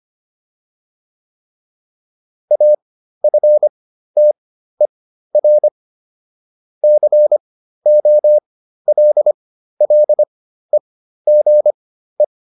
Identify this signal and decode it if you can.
Morse code, heard as a keyed tone, about 25 wpm, spaced as for 13 wpm, 600 Hz. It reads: AFTER COLLEGE